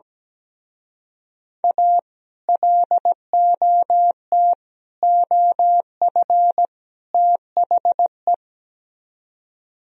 Morse code, heard as a keyed tone, about 17 wpm, 700 Hz. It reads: A LOT OF THE
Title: Speed-Racing - Sets of 4 Words 17wpm